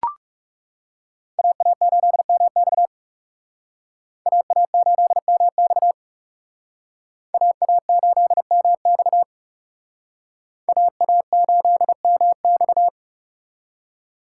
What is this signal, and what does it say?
Morse code, heard as a keyed tone, about 39 wpm, 700 Hz. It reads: AA8MX AA8MX AA8MX AA8MX